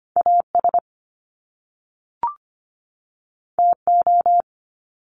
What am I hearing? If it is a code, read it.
Morse code, heard as a keyed tone, about 25 wpm, 700 Hz. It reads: AS TO
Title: Top Two Letter Combinations in Words 25wpm